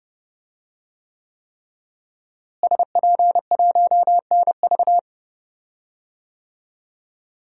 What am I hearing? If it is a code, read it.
Morse code, heard as a keyed tone, about 30 wpm, 700 Hz. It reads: SP1NV
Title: DXCC-100 Call Signs - 5 Characters - Repeated 3x - 30wpm